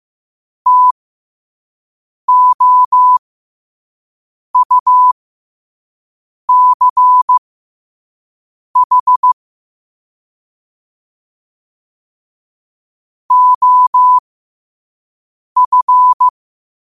Morse code, heard as a keyed tone, about 15 wpm, spaced as for 5 wpm, 1000 Hz. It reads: TOUCH OF
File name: qotdfiles / 5wpm